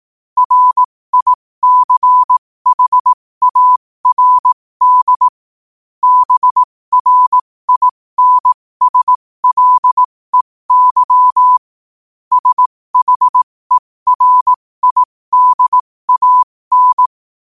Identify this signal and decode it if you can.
Morse code, heard as a keyed tone, about 18 wpm, spaced as for 15 wpm, 1000 Hz. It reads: RICHARD BRINSLEY SHERIDAN
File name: qotdfiles / 15wpm